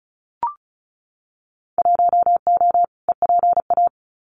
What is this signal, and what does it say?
Morse code, heard as a keyed tone, about 35 wpm, 700 Hz. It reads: 1O EPA
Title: Winter Field Day Exchanges 35wpm